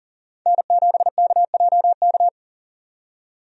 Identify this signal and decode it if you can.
Morse code, heard as a keyed tone, about 40 wpm, 700 Hz. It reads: N7KJK
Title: USA Call Signs - 5 Characters - 40wpm